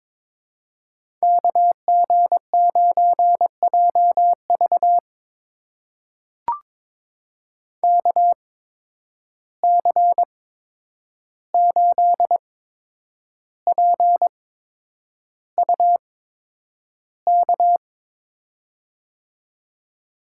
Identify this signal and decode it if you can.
Morse code, heard as a keyed tone, about 22 wpm, spaced as for 9 wpm, 700 Hz. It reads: KG9JV KC8PUK
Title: USA Call Signs - 8x Character Spacing - 22wpm